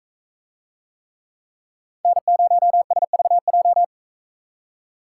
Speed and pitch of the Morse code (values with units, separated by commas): 40 wpm, 700 Hz